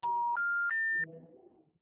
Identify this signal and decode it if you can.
Special information tone.